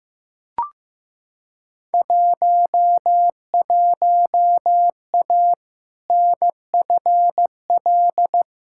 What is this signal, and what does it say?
Morse code, heard as a keyed tone, about 15 wpm, 700 Hz. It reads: 11A NFL
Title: Field Day Exchanges Encore - 15wpm